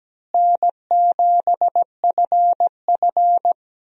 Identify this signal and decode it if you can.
Morse code, heard as a keyed tone, about 17 wpm, 700 Hz. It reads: N7FF
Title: USA Call Signs - 4 Characters - Repeated 2x - 17wpm